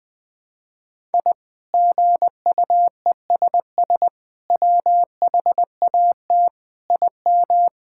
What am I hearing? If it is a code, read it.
Morse code, heard as a keyed tone, about 20 wpm, 700 Hz. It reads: I GUESS WHAT IM
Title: Sets of 8 Words - 20wpm